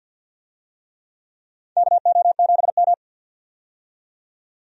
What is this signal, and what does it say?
Morse code, heard as a keyed tone, about 48 wpm, 700 Hz. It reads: KO7M
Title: USA Call Signs - 4 Characters - 50wpm